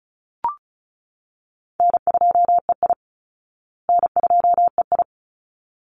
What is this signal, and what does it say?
Morse code, heard as a keyed tone, about 34 wpm, 700 Hz. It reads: N2EI N2EI
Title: Call Signs Repeated 3x - 35wpm